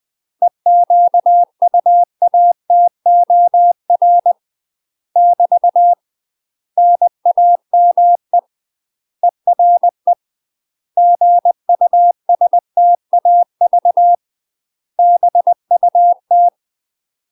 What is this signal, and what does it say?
Morse code, heard as a keyed tone, about 20 wpm, 700 Hz. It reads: EQUATOR <BT> NAME ERE GUSTAV BUT
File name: UK MASTER QSOs TRAINING / 20WPM / UK MASTER QSOs 0034